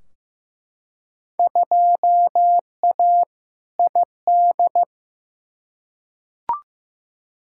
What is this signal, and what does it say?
Morse code, heard as a keyed tone, about 15 wpm, 700 Hz. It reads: 2A ID
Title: Field Day Exchanges - 15wpm